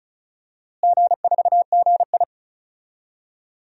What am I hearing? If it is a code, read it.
Morse code, heard as a keyed tone, about 34 wpm, 700 Hz. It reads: G4GI
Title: DXCC-100 Call Signs - 4 Characters - Repeated 3x - 35wpm